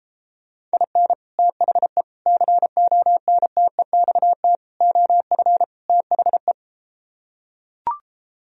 Morse code, heard as a keyed tone, about 33 wpm, 700 Hz. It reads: IN THE CONTEXT OF THE